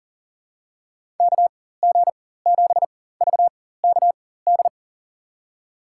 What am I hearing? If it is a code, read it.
Morse code, heard as a keyed tone, about 40 wpm, spaced as for 19 wpm, 700 Hz. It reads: KG7VKD
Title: USA Call Signs - 6 Characters - 4x Character Spacing - 40wpm